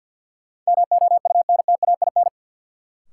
Morse code, heard as a keyed tone, about 46 wpm, 700 Hz. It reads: MOUNTAIN
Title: Sentences from Top 1000 Words - Review Part 1 - New Word Drill - 50wpm